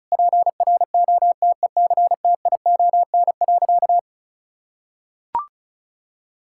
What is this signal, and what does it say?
Morse code, heard as a keyed tone, about 34 wpm, 700 Hz. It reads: PROTECTION.
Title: Sentences from Top 1600 Words 35wpm